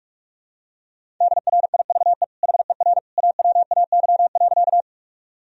Morse code, heard as a keyed tone, about 44 wpm, 700 Hz. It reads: DRIVE HER AWAY.
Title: Sentences from Top 1000 Words - Review Part 2 - 45wpm